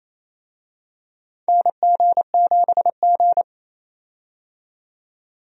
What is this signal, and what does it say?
Morse code, heard as a keyed tone, about 27 wpm, 700 Hz. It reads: NG7G
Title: USA Call Signs - 4 Characters - 28wpm